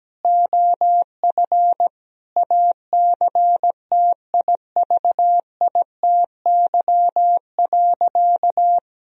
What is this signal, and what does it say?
Morse code, heard as a keyed tone, about 17 wpm, 700 Hz. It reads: OF ACTIVITY.
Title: Sentences from Top 1000 Words - Review Part 4 - 17wpm